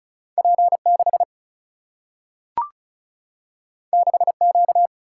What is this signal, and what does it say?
Morse code, heard as a keyed tone, about 36 wpm, 700 Hz. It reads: P6 6Q
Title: Single-Letter Single-Number 35wpm